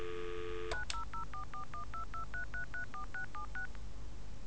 DTMF, heard as telephone touch tones, tones of about 110 ms. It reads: 900441223334676